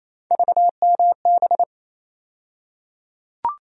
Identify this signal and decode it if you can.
Morse code, heard as a keyed tone, about 28 wpm, 700 Hz. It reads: VMB
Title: USA Call Sign Postfixes - 3 Letters 28wpm